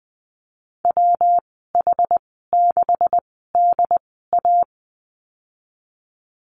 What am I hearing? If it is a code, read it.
Morse code, heard as a keyed tone, about 20 wpm, spaced as for 14 wpm, 700 Hz. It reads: WH6DA
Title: 140 Lesson 23g - Letter f - Call Signs 20wpm